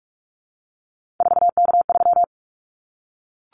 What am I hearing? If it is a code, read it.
Morse code, heard as a keyed tone, about 44 wpm, 700 Hz. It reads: 4K3
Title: Callsign Prefixes for Top 100 DXCC - 45wpm